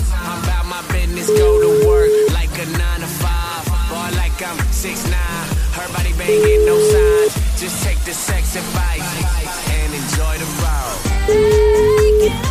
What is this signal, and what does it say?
Ringback tone (Europe, 425 Hz), with noise.